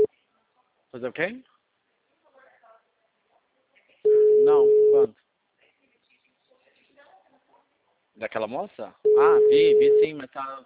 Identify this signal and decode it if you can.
Ringback tone (Europe, 425 Hz), with speech.
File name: out- -2007- - - .45150